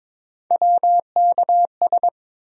Morse code, heard as a keyed tone, about 22 wpm, 700 Hz. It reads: WKS
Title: USA Call Sign Postfixes - 3 Letters 22wpm